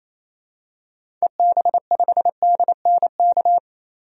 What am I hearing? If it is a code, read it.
Morse code, heard as a keyed tone, about 28 wpm, 700 Hz. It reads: EB5DNK
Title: DXCC-100 Call Signs - 6 Characters - 28wpm